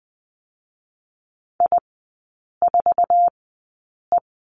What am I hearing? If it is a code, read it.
Morse code, heard as a keyed tone, about 20 wpm, spaced as for 8 wpm, 700 Hz. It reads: I4E
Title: 047 Lesson 9e - Number 4 - Three Characters 20wpm